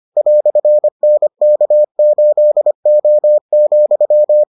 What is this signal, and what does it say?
Morse code, heard as a keyed tone, about 25 wpm, 600 Hz. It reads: "NK8O,